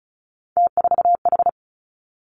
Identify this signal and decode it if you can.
Morse code, heard as a keyed tone, about 36 wpm, 700 Hz. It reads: T4H
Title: Three Challenging Letter-Number Combinations 35wpm